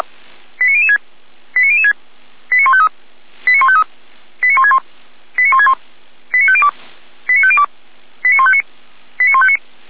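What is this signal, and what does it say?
Five-tone selective call (ZVEI), heard as quick runs of five tones, tones of about 70 ms. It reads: 89006 89006 89153 89153 89161 89161 89502 89502 89159 89159
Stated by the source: Five-tone ZVEI selective call 89160 received on SDR.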